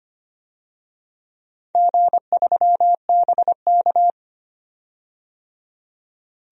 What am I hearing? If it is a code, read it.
Morse code, heard as a keyed tone, about 25 wpm, 700 Hz. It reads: G3BK
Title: DXCC-100 Call Signs - 4 Characters - 25wpm